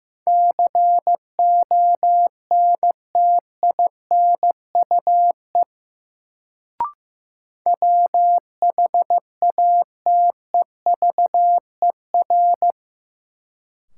Morse code, heard as a keyed tone, about 16 wpm, 700 Hz. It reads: CONTINUE WHATEVER
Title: Sentences from Top 1100 Words - New Word Drill Alt - 15wpm